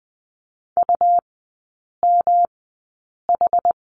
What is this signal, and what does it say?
Morse code, heard as a keyed tone, about 20 wpm, spaced as for 8 wpm, 700 Hz. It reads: UMH
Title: 107 Lesson 18e - Letter m - Three Characters 20wpm